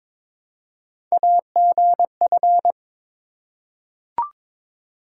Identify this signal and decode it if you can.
Morse code, heard as a keyed tone, about 22 wpm, 700 Hz. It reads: AGF
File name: practice / 22wpm / Three Letter-Number Combinations - 22wpm